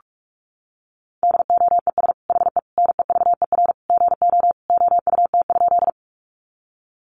Morse code, heard as a keyed tone, about 44 wpm, 700 Hz. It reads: DOES HE NEVER GO OUT?
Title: Sentences from Top 300 Words 45wpm